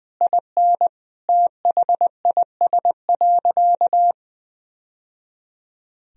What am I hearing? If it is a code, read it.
Morse code, heard as a keyed tone, about 19 wpm, 700 Hz. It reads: IN THIS.
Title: Sentences from Top 1000 Words - Review Part 4 - 20wpm